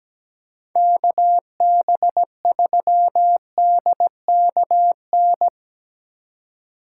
Morse code, heard as a keyed tone, about 19 wpm, 700 Hz. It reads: KB3DKN
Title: Speed Racing - USA 2x3 Call Signs 17wpm